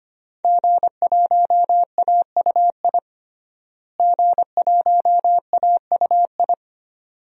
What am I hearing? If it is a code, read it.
Morse code, heard as a keyed tone, about 25 wpm, 700 Hz. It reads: G1AUI G1AUI
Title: DXCC-100 Call Signs - 5 Characters - Repeated 2x - 25wpm